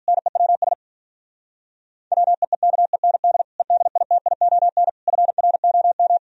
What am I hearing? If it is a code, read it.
Morse code, heard as a keyed tone, about 46 wpm, 700 Hz. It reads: NEWS WEEKEND EDITION FROM